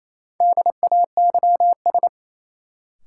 Morse code, heard as a keyed tone, about 28 wpm, 700 Hz. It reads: DAYS
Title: Sets of 5 Words Encore - New Word Drill - 28wpm